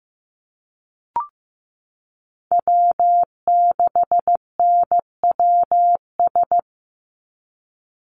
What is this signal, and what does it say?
Morse code, heard as a keyed tone, about 15 wpm, 700 Hz. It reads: W6NWS
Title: Call Signs Repeated 3x - 15wpm